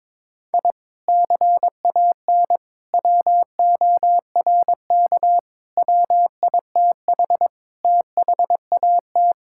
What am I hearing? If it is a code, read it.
Morse code, heard as a keyed tone, about 22 wpm, 700 Hz. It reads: I CAN WORK WITH THAT
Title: Sentences from Top 1000 Words - Review Part 5 - 22wpm